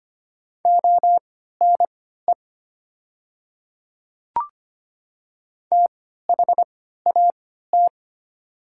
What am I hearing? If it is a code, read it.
Morse code, heard as a keyed tone, about 25 wpm, spaced as for 14 wpm, 700 Hz. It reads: ONE THAT
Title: 3x Character Spacing - Top 100 Words - 25wpm